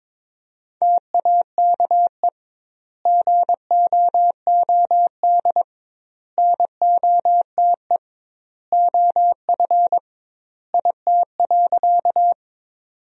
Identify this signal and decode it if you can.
Morse code, heard as a keyed tone, about 22 wpm, 700 Hz. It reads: TAKE GOOD NOTE OF IT.
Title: 2x Word-Spacing - Sentences from Top 500 Words - 22wpm